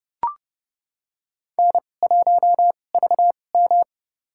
Morse code, heard as a keyed tone, about 31 wpm, spaced as for 22 wpm, 700 Hz. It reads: N1VM
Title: USA Call Signs - 4 Characters - 2x Character Spacing - 30wpm